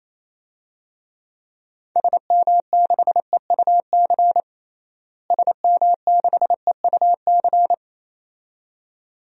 Morse code, heard as a keyed tone, about 27 wpm, 700 Hz. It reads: SM6EUC SM6EUC
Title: DXCC-100 Call Signs - 6 Characters - Repeated 2x - 28wpm